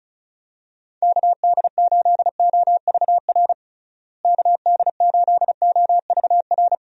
Morse code, heard as a keyed tone, about 36 wpm, 700 Hz. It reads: KD8OVR KD8OVR
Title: USA Call Signs - 6 Characters - Repeated 2x - 35wpm